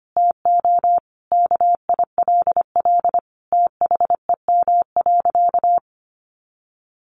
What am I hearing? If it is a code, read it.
Morse code, heard as a keyed tone, about 25 wpm, 700 Hz. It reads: TO KILL THEM.
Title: Additional Easy Phrases 25wpm